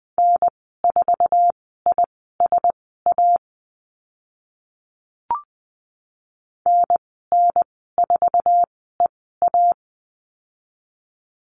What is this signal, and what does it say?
Morse code, heard as a keyed tone, about 20 wpm, spaced as for 14 wpm, 700 Hz. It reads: N4ISA NN4EA
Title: 055 Lesson 10g - Letter r - Call Signs 20wpm